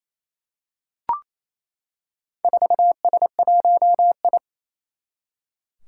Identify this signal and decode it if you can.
Morse code, heard as a keyed tone, about 29 wpm, 700 Hz. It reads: 4S1I
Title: Four Challenging Letter-Number Combinations 28wpm